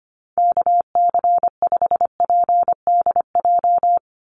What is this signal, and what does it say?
Morse code, heard as a keyed tone, about 25 wpm, 700 Hz. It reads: KC5PDJ
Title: USA Call Signs 25wpm